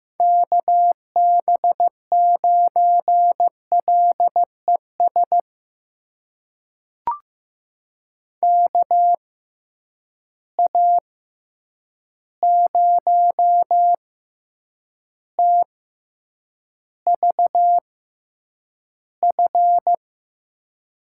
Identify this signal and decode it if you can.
Morse code, heard as a keyed tone, about 15 wpm, spaced as for 8 wpm, 700 Hz. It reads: KB9LES KA0TVF
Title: USA Call Signs - 6 Characters - 6x Character Spacing - 15wpm